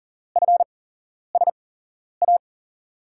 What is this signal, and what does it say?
Morse code, heard as a keyed tone, about 40 wpm, spaced as for 11 wpm, 700 Hz. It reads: FSA
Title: USA Call Signs - 8x Character Spacing - 40wpm